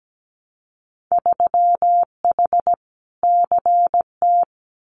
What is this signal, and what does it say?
Morse code, heard as a keyed tone, about 17 wpm, 700 Hz. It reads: 3H CT
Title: Winter Field Day Exchanges Encore 17wpm